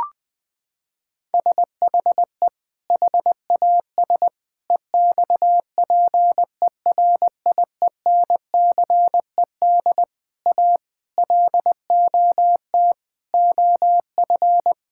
Morse code, heard as a keyed tone, about 20 wpm, 700 Hz. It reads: SHE HAS EXPERIENCED A LOT OF
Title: Sentences from Top 1600 Words 20wpm